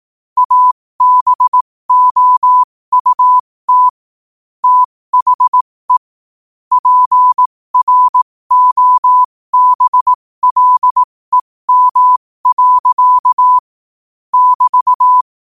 Morse code, heard as a keyed tone, about 18 wpm, spaced as for 15 wpm, 1000 Hz. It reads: ABOUT THE PROBLEM. <BT>
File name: qotdfiles / 15wpm